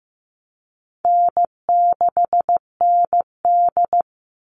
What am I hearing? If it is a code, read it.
Morse code, heard as a keyed tone, about 15 wpm, 700 Hz. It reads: N6ND
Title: All Cut Numbers - 4-Digit 15wpm